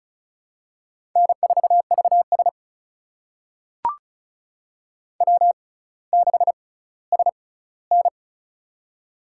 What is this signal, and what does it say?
Morse code, heard as a keyed tone, about 34 wpm, spaced as for 18 wpm, 700 Hz. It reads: N4VS W6SN
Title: USA Call Signs - 4 Characters - 6x Character Spacing - 35wpm